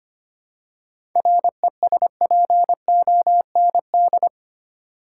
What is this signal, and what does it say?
Morse code, heard as a keyed tone, about 25 wpm, 700 Hz. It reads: RESPOND
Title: Seven Letter Words - 25wpm